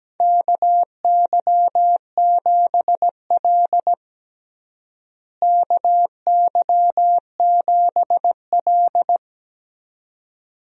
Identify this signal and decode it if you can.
Morse code, heard as a keyed tone, about 17 wpm, 700 Hz. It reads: KY7L KY7L
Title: USA Call Signs - 4 Characters - Repeated 2x - 17wpm